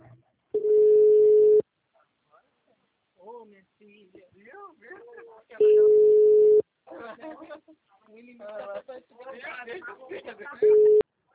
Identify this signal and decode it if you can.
Ringback tone (Europe, 425 Hz), with speech.